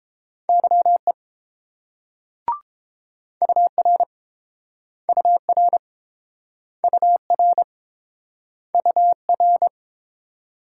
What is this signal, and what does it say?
Morse code, heard as a keyed tone, about 30 wpm, 690 Hz. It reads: YE UR UR UR UR